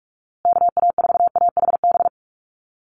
Morse code, heard as a keyed tone, about 44 wpm, 700 Hz. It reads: KA4AHB
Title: USA Call Signs Repeated 2x 45wpm